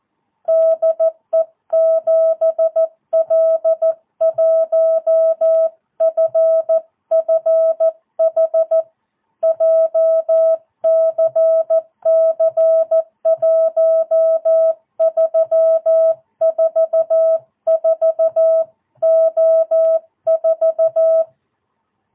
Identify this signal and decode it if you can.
Morse code, heard as a keyed tone, about 14 wpm, 660 Hz. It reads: DE 7L1FFH JCC 1344O4
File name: jh8cbh / 2022cw